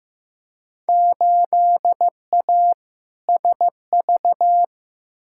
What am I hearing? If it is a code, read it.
Morse code, heard as a keyed tone, about 15 wpm, 700 Hz. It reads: 8A SV
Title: Field Day Exchanges Encore - 15wpm